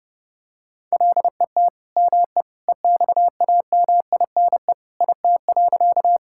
Morse code, heard as a keyed tone, about 30 wpm, 700 Hz. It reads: LET ME EXAMINE IT.
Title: Sentences from Top 1600 Words 30wpm